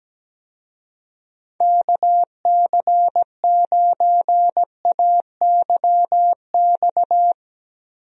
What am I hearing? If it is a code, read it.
Morse code, heard as a keyed tone, about 17 wpm, 700 Hz. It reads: KC9AYX
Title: USA Call Signs - 6 Characters - 17wpm